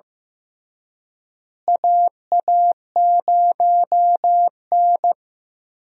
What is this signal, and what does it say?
Morse code, heard as a keyed tone, about 15 wpm, 700 Hz. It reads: AA0N
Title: USA Call Signs - 4 Characters - Repeated 2x - 15wpm